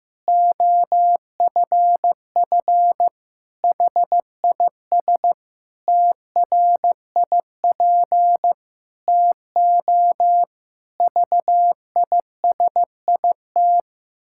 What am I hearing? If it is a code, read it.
Morse code, heard as a keyed tone, about 15 wpm, 700 Hz. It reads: OFF HIS TRIP TO VISIT
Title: Pangram Challenge v2 - 15wpm